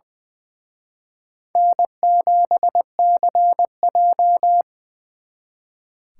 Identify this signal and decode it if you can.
Morse code, heard as a keyed tone, about 23 wpm, 700 Hz. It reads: N7CJ
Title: Speed Racing - USA 1x2 Call Signs 20wpm